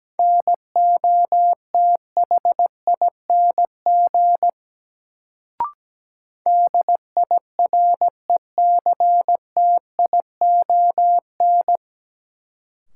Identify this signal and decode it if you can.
Morse code, heard as a keyed tone, about 17 wpm, 700 Hz. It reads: NOTHING DIRECTION
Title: Sets of 4 Words - New Word Drill - 17wpm